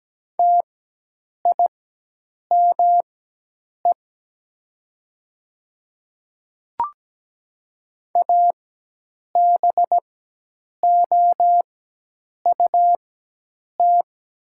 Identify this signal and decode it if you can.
Morse code, heard as a keyed tone, about 17 wpm, spaced as for 8 wpm, 700 Hz. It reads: TIME ABOUT
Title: 4x Character Spacing - Top 100 Words - 17wpm